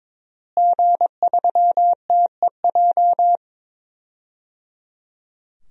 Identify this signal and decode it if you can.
Morse code, heard as a keyed tone, about 22 wpm, 700 Hz. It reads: G3TEJ
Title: DXCC-100 Call Signs - 5 Characters - Repeated 3x - 22wpm